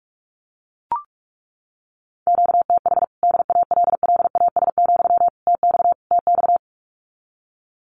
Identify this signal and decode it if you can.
Morse code, heard as a keyed tone, about 44 wpm, 700 Hz. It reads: QTH DALLAS, TX TX
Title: Common QSO Phrases Encore 45wpm